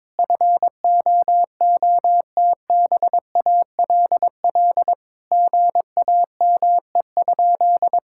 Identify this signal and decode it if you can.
Morse code, heard as a keyed tone, about 22 wpm, 700 Hz. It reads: FOOTBALL GAME?
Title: Sentences from Top 1600 Words 22wpm